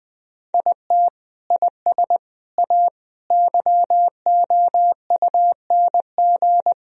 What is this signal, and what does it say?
Morse code, heard as a keyed tone, about 20 wpm, 700 Hz. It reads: IT IS A YOUNG
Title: Sentences from Top 1000 Words - Review Part 4 - 20wpm